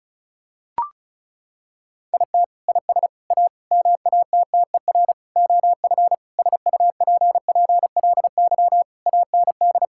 Morse code, heard as a keyed tone, about 36 wpm, 690 Hz. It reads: IT IS A MATTER OF SUPPLY AND
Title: Sentences from Top 1000 Words - Review Part 5 - 35wpm